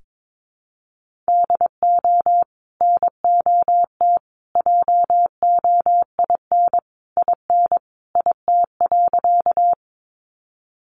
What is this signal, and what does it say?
Morse code, heard as a keyed tone, about 22 wpm, 700 Hz. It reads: DO NOT JOIN IN IT.
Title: Sentences from Top 1400 Words 22wpm